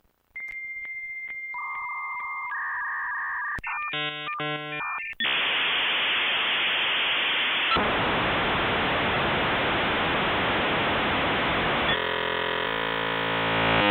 Dial-up modem handshake.